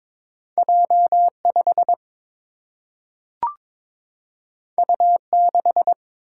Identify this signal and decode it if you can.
Morse code, heard as a keyed tone, about 22 wpm, 700 Hz. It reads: J 5 U 6
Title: Single-Letter Single-Number 22wpm